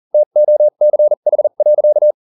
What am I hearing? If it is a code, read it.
Morse code, heard as a keyed tone, about 40 wpm, spaced as for 35 wpm, 600 Hz. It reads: TOCH.